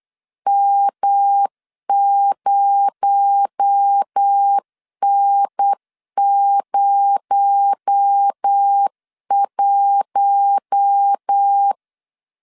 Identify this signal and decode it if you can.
Morse code, heard as a keyed tone, about 8 wpm, 800 Hz. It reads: M0N01
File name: LEZIONI / 6 PKX0 / 6 PKX0 8wpm 3341